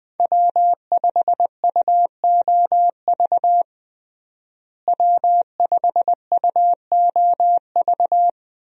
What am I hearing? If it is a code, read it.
Morse code, heard as a keyed tone, about 20 wpm, 700 Hz. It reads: W5UOV W5UOV
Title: USA Call Signs - 5 Characters - Repeated 2x - 20wpm